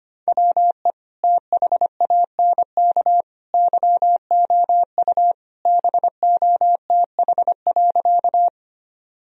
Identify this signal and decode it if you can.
Morse code, heard as a keyed tone, about 25 wpm, 700 Hz. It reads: WE THANK YOU BOTH.
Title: Sentences from Top 1000 Words - Review Part 1 - 25wpm